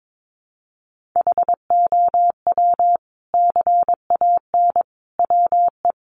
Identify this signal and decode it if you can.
Morse code, heard as a keyed tone, about 22 wpm, 700 Hz. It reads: HOW CAN WE
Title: Sentences from Top 1400 Words 22wpm